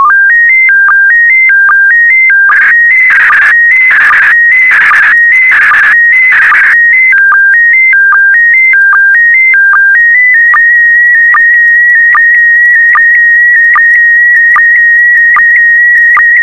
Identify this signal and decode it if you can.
SSTV (slow-scan television), PD 160.
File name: sstv / PD160-VK4MSL